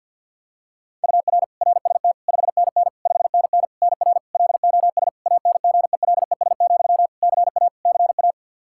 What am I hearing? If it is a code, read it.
Morse code, heard as a keyed tone, about 48 wpm, 700 Hz. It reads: UR RST 5NN 5NN NR LOS ANGELES, CA CA
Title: K1USN SST QSO Practice 50wpm